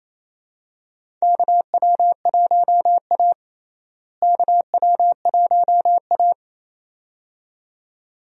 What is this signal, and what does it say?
Morse code, heard as a keyed tone, about 28 wpm, 700 Hz. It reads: KW1A KW1A
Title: USA Call Signs - 4 Characters - Repeated 2x - 28wpm